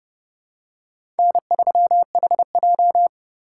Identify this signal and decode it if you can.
Morse code, heard as a keyed tone, about 30 wpm, 700 Hz. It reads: N3HJ
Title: USA Call Signs - 4 Characters - 2x Character Spacing - 30wpm